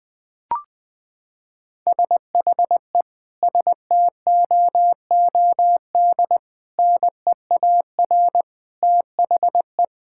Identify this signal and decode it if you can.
Morse code, heard as a keyed tone, about 20 wpm, 700 Hz. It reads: SHE STOOD NEAR THE
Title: Sentences from Top 1600 Words 20wpm